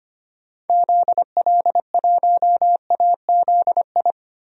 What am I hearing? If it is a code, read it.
Morse code, heard as a keyed tone, about 25 wpm, 700 Hz. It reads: ZL1AZI
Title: DXCC-100 Call Signs - 6 Characters - 25wpm